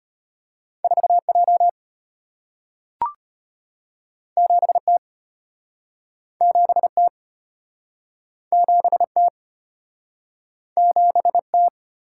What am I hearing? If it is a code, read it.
Morse code, heard as a keyed tone, about 33 wpm, 700 Hz. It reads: 4J 7T 7T 7T 7T